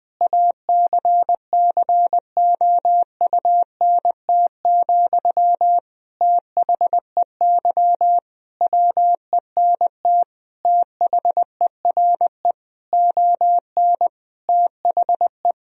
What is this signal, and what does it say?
Morse code, heard as a keyed tone, about 20 wpm, 700 Hz. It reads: ACCOUNT, THEY WENT THERE ON THE
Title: Sentences from Top 1000 Words - Review Part 4 - 20wpm